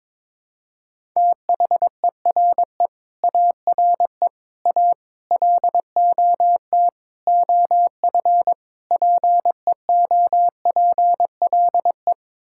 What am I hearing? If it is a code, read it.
Morse code, heard as a keyed tone, about 22 wpm, 700 Hz. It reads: THERE ARE A LOT OF PEOPLE
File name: practice / 22wpm / Sets of 6 Words - 22wpm